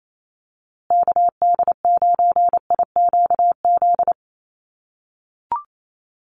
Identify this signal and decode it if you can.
Morse code, heard as a keyed tone, about 28 wpm, 700 Hz. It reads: KD9IQZ
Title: USA Call Signs Repeated 2x 28wpm